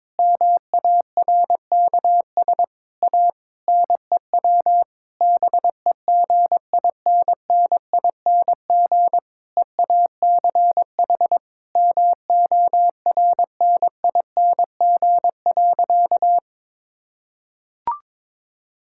Morse code, heard as a keyed tone, about 22 wpm, 700 Hz. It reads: MARKS A NEW BEGINNING EACH MORNING.
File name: Difficult Sentences with Hints - 22wpm